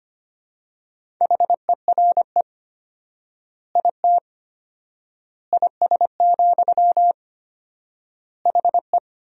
Morse code, heard as a keyed tone, about 25 wpm, 700 Hz. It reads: HERE IT IS, HE
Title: 4x Word-Spacing - Sentences from Top 200 Words - 25wpm